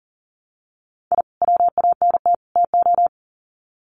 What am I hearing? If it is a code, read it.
Morse code, heard as a keyed tone, about 40 wpm, 700 Hz. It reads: I WANT TO
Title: Sets of 3 Words 40wpm